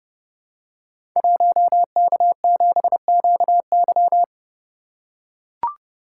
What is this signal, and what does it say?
Morse code, heard as a keyed tone, about 30 wpm, 700 Hz. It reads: 1K7QY
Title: Code Groups - 5 Letter:Numbers - 30wpm